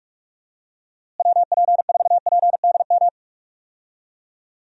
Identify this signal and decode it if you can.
Morse code, heard as a keyed tone, about 42 wpm, 700 Hz. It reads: WP4PDM
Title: USA Call Signs - 6 Characters - Repeated 3x - 45wpm